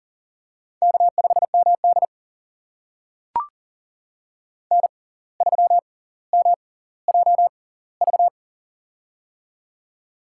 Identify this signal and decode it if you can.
Morse code, heard as a keyed tone, about 38 wpm, spaced as for 19 wpm, 700 Hz. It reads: K5MD N3MJV
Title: USA Call Signs - 6x Character Spacing - 40wpm